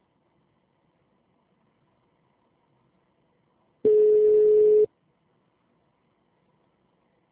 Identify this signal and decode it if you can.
Ringback tone (Europe, 425 Hz).
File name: out- -2013- - - .42082